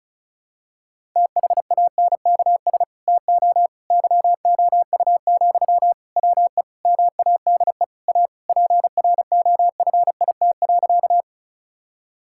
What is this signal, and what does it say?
Morse code, heard as a keyed tone, about 36 wpm, 700 Hz. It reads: THANKS TO YOU, WE MADE A PROFIT.
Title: Sentences from Top 1000 Words - Review Part 5 - 35wpm